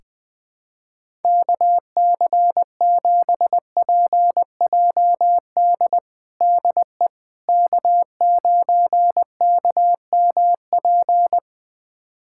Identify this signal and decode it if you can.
Morse code, heard as a keyed tone, about 20 wpm, 700 Hz. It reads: KC7PJD DE K9KMP
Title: USA Callsign DE Callsign 20wpm